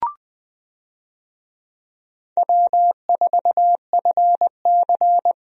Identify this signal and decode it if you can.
Morse code, heard as a keyed tone, about 20 wpm, 700 Hz. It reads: W4FC
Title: USA Call Signs - 4 Characters - Repeated 2x - 20wpm